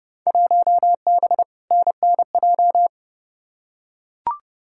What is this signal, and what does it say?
Morse code, heard as a keyed tone, about 30 wpm, 700 Hz. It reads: 1B NNJ
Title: Field Day Exchanges Encore - 30wpm